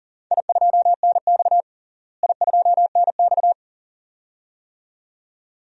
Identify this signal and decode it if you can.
Morse code, heard as a keyed tone, about 37 wpm, 700 Hz. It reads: I2NX I2NX
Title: DXCC-100 Call Signs - 4 Characters - Repeated 2x - 40wpm